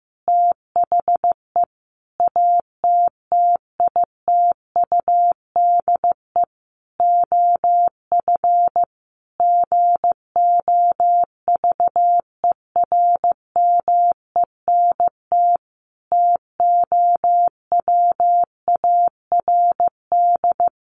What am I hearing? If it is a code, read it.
Morse code, heard as a keyed tone, about 14 wpm, 700 Hz. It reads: THE ATTITUDE OF GOVERMENT TOWARD
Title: Sentences from Top 1300 Words 15wpm